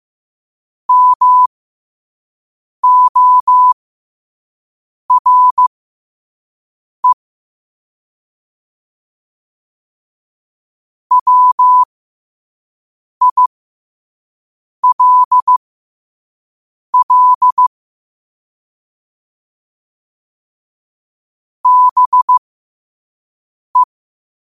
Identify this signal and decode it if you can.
Morse code, heard as a keyed tone, about 15 wpm, spaced as for 5 wpm, 1000 Hz. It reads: MORE WILL BE
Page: New quotes every day in morse code at 5 Words per minute.